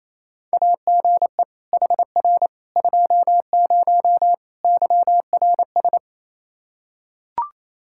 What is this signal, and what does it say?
Morse code, heard as a keyed tone, about 27 wpm, 700 Hz. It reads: AGE HR 20 YRS
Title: Practice QSO - Exchange 3 - More Info - 28wpm